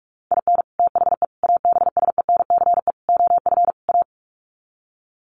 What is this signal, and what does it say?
Morse code, heard as a keyed tone, about 44 wpm, 700 Hz. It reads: IN THE ABSENCE OF A
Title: Sets of 5 Words Encore 45wpm